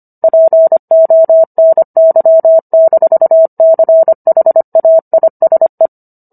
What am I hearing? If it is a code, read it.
Morse code, heard as a keyed tone, about 25 wpm, 650 Hz. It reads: PONY-CHAISE